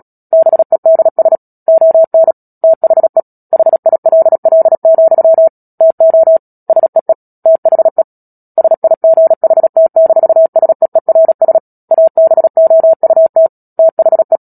Morse code, heard as a keyed tone, about 37 wpm, 650 Hz. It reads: BEDS ON THE HILL, TO SEE THE SIGHT-SEERS ABOUT THE
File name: Book Two / 10-Chapter 10 - Epilogue - 36 WPM